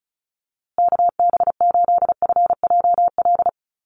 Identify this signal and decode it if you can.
Morse code, heard as a keyed tone, about 34 wpm, 700 Hz. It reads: KB8FJL
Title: USA Call Signs Repeated 3x 35wpm